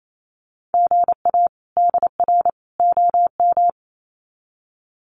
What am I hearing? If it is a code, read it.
Morse code, heard as a keyed tone, about 28 wpm, 700 Hz. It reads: GA DR OM
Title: Common QSO Phrases 28wpm